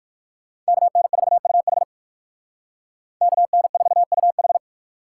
Morse code, heard as a keyed tone, about 58 wpm, 700 Hz. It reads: KN4UH KN4UH